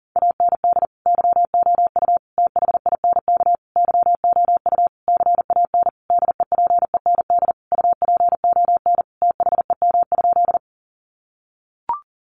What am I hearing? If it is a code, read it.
Morse code, heard as a keyed tone, about 40 wpm, 700 Hz. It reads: AND YOU THINK YOU CAN DEPEND UPON THEM?
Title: Sentences from Top 1300 Words 40wpm